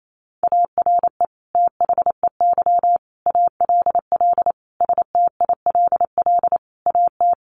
Morse code, heard as a keyed tone, about 28 wpm, 700 Hz. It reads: ARE THEY ALL STILL AT
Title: Sentences from Top 1500 Words 28wpm